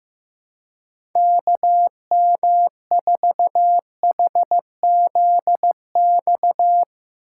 Morse code, heard as a keyed tone, about 15 wpm, 700 Hz. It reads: KM4HZX
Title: USA Call Signs - 6 Characters - 6x Character Spacing - 15wpm